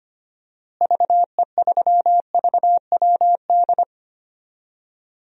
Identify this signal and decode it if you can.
Morse code, heard as a keyed tone, about 25 wpm, 700 Hz. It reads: VE3VWD
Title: DXCC-100 Call Signs - 6 Characters - Repeated 3x - 25wpm